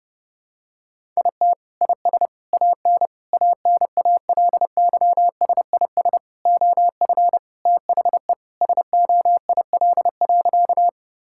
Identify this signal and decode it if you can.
Morse code, heard as a keyed tone, about 32 wpm, 700 Hz. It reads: IT IS AN ANALYSIS OF THE SOIL.
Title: Sentences from Top 1000 Words - Review Part 5 - 30wpm